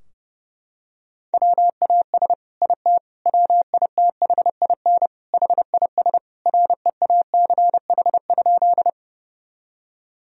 Morse code, heard as a keyed tone, about 32 wpm, 700 Hz. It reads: WAS IT WITHIN HIS REACH?
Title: Sentences from Top 1000 Words - Review Part 4 - 30wpm